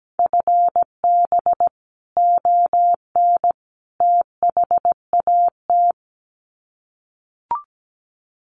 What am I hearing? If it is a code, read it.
Morse code, heard as a keyed tone, about 17 wpm, 700 Hz. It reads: FB ON THAT
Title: Common QSO Phrases 17wpm